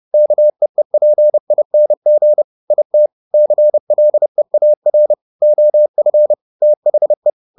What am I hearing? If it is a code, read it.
Morse code, heard as a keyed tone, about 30 wpm, 600 Hz. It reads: KEEPING IT CLEAR OF THE
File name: the old man and the sea / the old man and the sea 1 30wpm